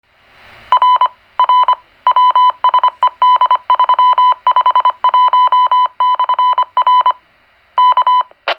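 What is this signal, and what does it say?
Morse code, heard as a keyed tone, about 25 wpm, 1000 Hz. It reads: R R WSED351/R K
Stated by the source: CW identification of a GMRS repeater.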